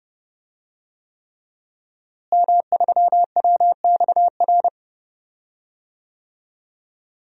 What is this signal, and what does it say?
Morse code, heard as a keyed tone, about 32 wpm, 700 Hz. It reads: M3WXR
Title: DXCC-100 Call Signs - 5 Characters - Repeated 3x - 30wpm